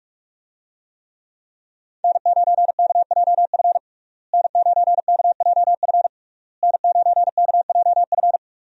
Morse code, heard as a keyed tone, about 41 wpm, 700 Hz. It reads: N9KJF N9KJF N9KJF
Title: USA Call Signs - 5 Characters - Repeated 3x - 45wpm